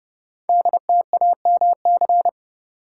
Morse code, heard as a keyed tone, about 30 wpm, 700 Hz. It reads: DTAMC
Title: Code Groups - 5 Letters - 30wpm